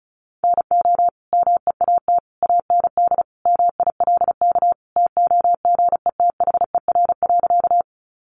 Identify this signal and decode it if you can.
Morse code, heard as a keyed tone, about 34 wpm, 700 Hz. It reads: NO MEAT AND MILK TOGETHER.
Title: Sentences from Top 1500 Words 35wpm